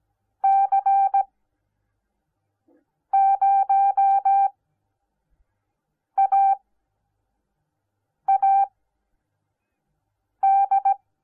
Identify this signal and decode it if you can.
Morse code, heard as a keyed tone, about 17 wpm, spaced as for 4 wpm, 780 Hz. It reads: C0AAD